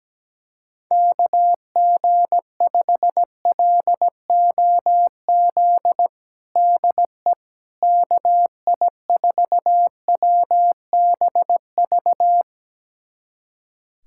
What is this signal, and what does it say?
Morse code, heard as a keyed tone, about 17 wpm, 700 Hz. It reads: KG5LOZ DE KI4WBV
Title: USA Callsign DE Callsign - 6 Characters - 17wpm